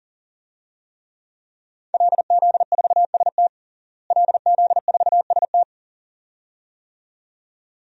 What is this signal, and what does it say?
Morse code, heard as a keyed tone, about 41 wpm, 700 Hz. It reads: LZ4ST LZ4ST
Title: DXCC-100 Call Signs - 5 Characters - Repeated 2x - 40wpm